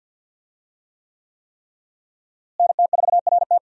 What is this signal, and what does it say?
Morse code, heard as a keyed tone, about 49 wpm, 700 Hz. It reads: NT4RT
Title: USA Call Signs - 5 Characters - Repeated 3x - 50wpm